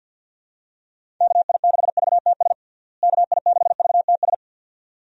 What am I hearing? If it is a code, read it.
Morse code, heard as a keyed tone, about 51 wpm, 700 Hz. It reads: KI6VTS KI6VTS
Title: USA Call Signs - 6 Characters - Repeated 2x - 50wpm